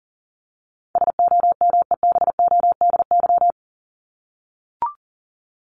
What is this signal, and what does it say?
Morse code, heard as a keyed tone, about 40 wpm, 700 Hz. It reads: SOMEBODY
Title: Top 1751-2000 Words 40wpm